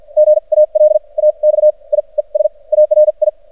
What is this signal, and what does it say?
Morse code, heard as a keyed tone, about 53 wpm, spaced as for 36 wpm, 610 Hz. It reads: GALAXIES ARI
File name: archived / CRSnet